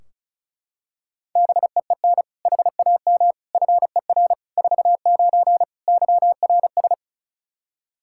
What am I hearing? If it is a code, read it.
Morse code, heard as a keyed tone, about 34 wpm, 700 Hz. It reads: BEEN HAM FER 49 YRS
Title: K1USN SST QSO Practice 35wpm